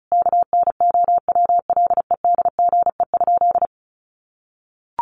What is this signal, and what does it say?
Morse code, heard as a keyed tone, about 34 wpm, 700 Hz. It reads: KNOWLEDGE?
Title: Sentences from Top 600 Words 35wpm